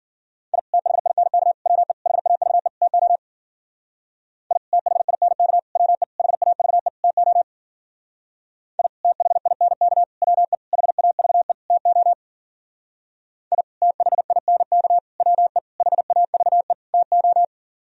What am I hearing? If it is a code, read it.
Morse code, heard as a keyed tone, about 50 wpm, 700 Hz. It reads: I THINK WE HAVE TO I THINK WE HAVE TO I THINK WE HAVE TO I THINK WE HAVE TO
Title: Speed-Racing - Sets of 5 Words 40wpm